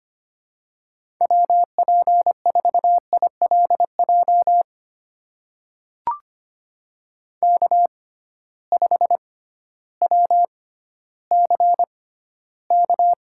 Morse code, heard as a keyed tone, about 25 wpm, spaced as for 14 wpm, 700 Hz. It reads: WP4ILJ K5WCK
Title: USA Call Signs - 6x Character Spacing - 25wpm